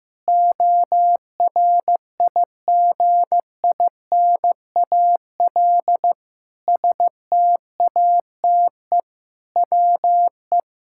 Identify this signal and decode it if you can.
Morse code, heard as a keyed tone, about 16 wpm, 700 Hz. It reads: ORIGINAL STATE WE
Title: Sentences from Top 1000 Words - Review Part 6 - 15wpm